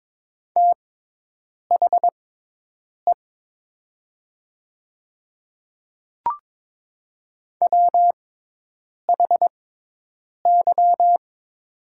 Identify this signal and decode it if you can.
Morse code, heard as a keyed tone, about 22 wpm, spaced as for 8 wpm, 700 Hz. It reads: THE WHY
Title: 6x Character Spacing - Top 500 Words - 22wpm